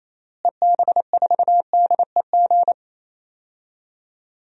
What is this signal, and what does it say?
Morse code, heard as a keyed tone, about 28 wpm, 700 Hz. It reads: EB4DEG
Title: DXCC-100 Call Signs - 6 Characters - Repeated 2x - 28wpm